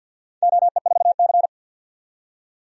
Morse code, heard as a keyed tone, about 53 wpm, 700 Hz. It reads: OE4C
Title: DXCC-100 Call Signs - 4 Characters - 50wpm